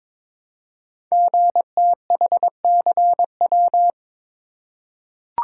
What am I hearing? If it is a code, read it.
Morse code, heard as a keyed tone, about 22 wpm, 700 Hz. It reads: GTHCW
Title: Code Groups - 5 Letters - 22wpm